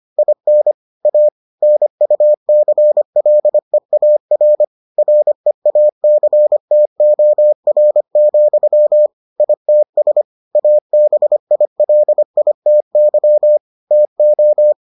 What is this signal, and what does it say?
Morse code, heard as a keyed tone, about 25 wpm, 600 Hz. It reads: IN A NUCLEAR REACTOR, ITS ABILITY TO